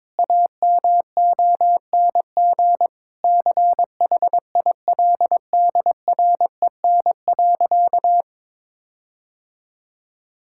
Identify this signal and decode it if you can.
Morse code, heard as a keyed tone, about 22 wpm, 700 Hz. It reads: AMONG CHILDREN.
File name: Difficult Sentences with Hints - 22wpm